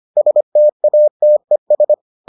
Morse code, heard as a keyed tone, about 25 wpm, 600 Hz. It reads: STATES